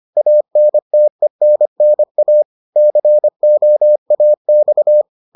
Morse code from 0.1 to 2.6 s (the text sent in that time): ANTENNA